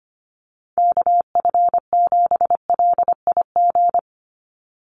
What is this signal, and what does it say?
Morse code, heard as a keyed tone, about 25 wpm, 700 Hz. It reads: KF7LIG
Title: USA Call Signs 25wpm